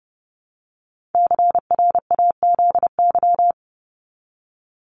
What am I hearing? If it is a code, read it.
Morse code, heard as a keyed tone, about 30 wpm, 700 Hz. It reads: CRAZY
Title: Top 1501-1750 Words 30wpm